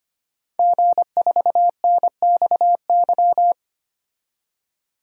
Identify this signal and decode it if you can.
Morse code, heard as a keyed tone, about 25 wpm, 700 Hz. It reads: G4NXY
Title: DXCC-100 Call Signs - 5 Characters - 25wpm